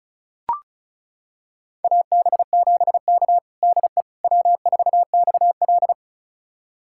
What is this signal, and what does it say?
Morse code, heard as a keyed tone, about 36 wpm, 700 Hz. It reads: AB7K DE W4XL
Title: USA Callsign DE Callsign - 4 Characters - 35wpm